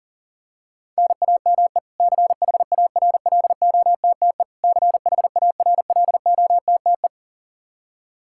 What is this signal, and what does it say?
Morse code, heard as a keyed tone, about 40 wpm, 700 Hz. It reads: NAME CHARLOTTE CHARLOTTE
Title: Practice QSO - Exchange 1 - Beginning - 40wpm